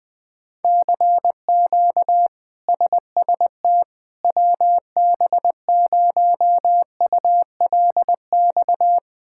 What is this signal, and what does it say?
Morse code, heard as a keyed tone, about 21 wpm, 700 Hz. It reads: CQ SST WB0ULX
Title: K1USN SST QSO Practice - 20wpm